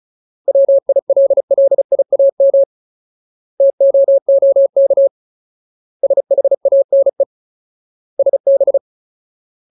Morse code, heard as a keyed tone, about 34 wpm, 550 Hz. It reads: WILLIAM TOOK SHANE SB
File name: Ultimate Word List / 35WPM Ultimate